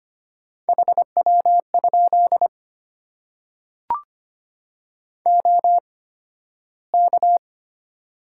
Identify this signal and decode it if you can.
Morse code, heard as a keyed tone, about 25 wpm, spaced as for 13 wpm, 700 Hz. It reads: HW? OK
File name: Top 90 QSO Elements - 8x Character Spacing - 25wpm